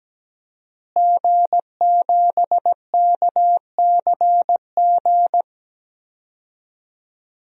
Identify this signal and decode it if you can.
Morse code, heard as a keyed tone, about 17 wpm, 700 Hz. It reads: G7KCG
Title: DXCC-100 Call Signs - 5 Characters - Repeated 3x - 17wpm